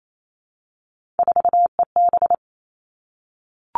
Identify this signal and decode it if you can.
Morse code, heard as a keyed tone, about 28 wpm, 700 Hz. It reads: 4EB
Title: All Cut Numbers - 3-Digit 28wpm